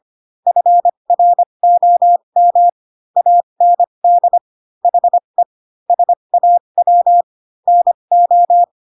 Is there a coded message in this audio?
Yes: Morse code from 0.5 to 8.6 s, heard as a keyed tone, about 25 wpm, spaced as for 22 wpm, 700 Hz. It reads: FROM AND HE SAW NO